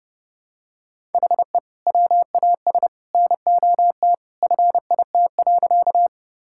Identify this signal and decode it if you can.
Morse code, heard as a keyed tone, about 30 wpm, 700 Hz. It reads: HE WAS NOT FIT.
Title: Sentences from Top 1000 Words - Review Part 5 - 30wpm